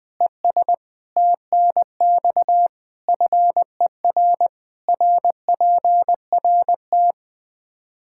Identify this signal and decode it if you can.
Morse code, heard as a keyed tone, about 20 wpm, 700 Hz. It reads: ES TNX FER RPRT
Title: Practice QSO - Exchange 2 - Info - 20wpm